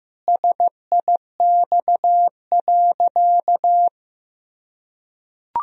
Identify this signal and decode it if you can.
Morse code, heard as a keyed tone, about 15 wpm, 700 Hz. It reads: SIX.
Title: Sentences from Top 1000 Words - Review Part 2 - 15wpm